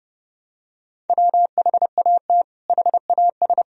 Morse code, heard as a keyed tone, about 32 wpm, 700 Hz. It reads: WHAT HAS
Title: Sentences from Top 100 Words and Abbreviations 30wpm